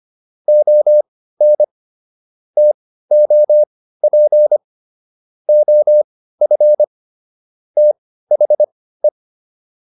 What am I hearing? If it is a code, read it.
Morse code, heard as a keyed tone, about 25 wpm, spaced as for 15 wpm, 600 Hz. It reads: ON TOP OF THE